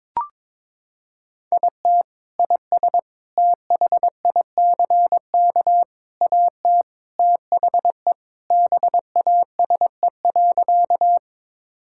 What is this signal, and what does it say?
Morse code, heard as a keyed tone, about 21 wpm, 700 Hz. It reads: IT IS THICK AT THE BASE.
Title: Sentences from Top 1600 Words 22wpm